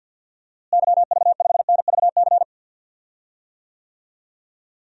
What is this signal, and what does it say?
Morse code, heard as a keyed tone, about 52 wpm, 700 Hz. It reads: YV5NVC
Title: DXCC-100 Call Signs - 6 Characters - Repeated 3x - 50wpm